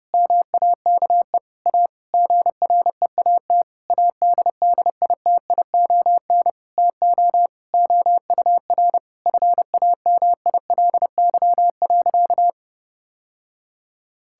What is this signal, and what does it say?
Morse code, heard as a keyed tone, about 32 wpm, 700 Hz. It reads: MAKE A GREAT ADDITION TO OUR FAMILY.
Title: Sentences from Top 1000 Words - Review Part 6 - 30wpm